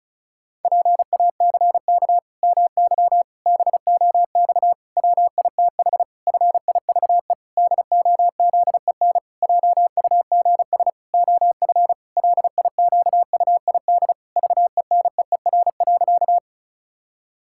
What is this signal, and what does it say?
Morse code, heard as a keyed tone, about 34 wpm, 700 Hz. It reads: PACK MY BOX WITH FIVE DOZEN JUGS OF LIQUID VENEER.
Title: Pangram Challenge v2 - 35wpm